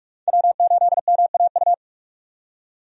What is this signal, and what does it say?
Morse code, heard as a keyed tone, about 41 wpm, 700 Hz. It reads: W8MAU
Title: USA Call Signs - 5 Characters - 45wpm